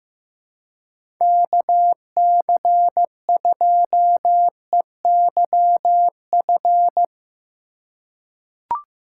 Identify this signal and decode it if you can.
Morse code, heard as a keyed tone, about 15 wpm, 700 Hz. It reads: KC2EYF
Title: USA Call Signs - 4x Character Spacing - 15wpm